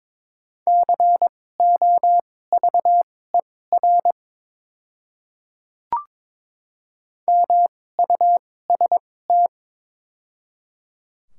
Morse code, heard as a keyed tone, about 22 wpm, spaced as for 16 wpm, 700 Hz. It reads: COVER MUST
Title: 2x Character Spacing - Top 300 Words - 22wpm